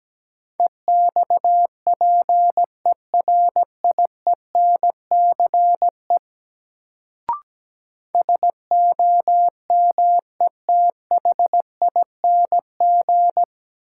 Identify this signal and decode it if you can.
Morse code, heard as a keyed tone, about 17 wpm, 700 Hz. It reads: EXPERIENCE SOMETHING
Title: Sentences from Top 600 Words - New Word Drill - 17wpm